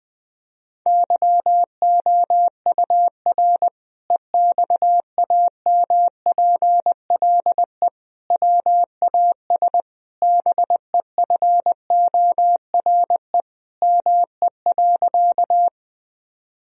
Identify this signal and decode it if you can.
Morse code, heard as a keyed tone, about 20 wpm, 700 Hz. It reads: YOUR EXAMPLE WAS BEFORE ME.
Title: Sentences from Top 1000 Words - Review Part 1 - 20wpm